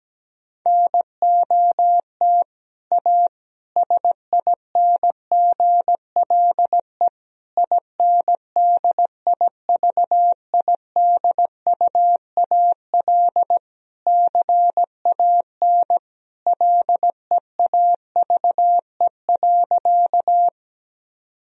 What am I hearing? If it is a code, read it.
Morse code, heard as a keyed tone, about 17 wpm, 700 Hz. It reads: NOT A SINGLE INDIVIDUAL CAN LEAVE.
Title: Sentences from Top 1000 Words - Review Part 4 - 17wpm